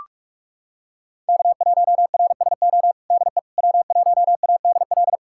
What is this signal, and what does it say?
Morse code, heard as a keyed tone, about 46 wpm, 700 Hz. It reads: K1RSO DE W1ADL
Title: USA Callsign DE Callsign - 5 Characters - 45wpm